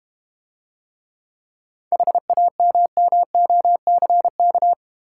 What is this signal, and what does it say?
Morse code, heard as a keyed tone, about 31 wpm, 700 Hz. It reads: HAMMOCK
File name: Easy Words - Speed-Racing with Hints - 25wpm